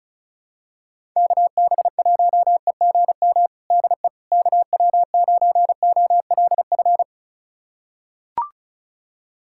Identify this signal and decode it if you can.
Morse code, heard as a keyed tone, about 34 wpm, 700 Hz. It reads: KB1EGM DE KW9OLF
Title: USA Callsign DE Callsign 35wpm